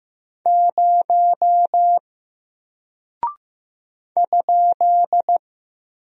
Morse code, heard as a keyed tone, about 15 wpm, 700 Hz. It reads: 0 ?
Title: Single Character (Letter:Number:Punctuation:Prosign) 15wpm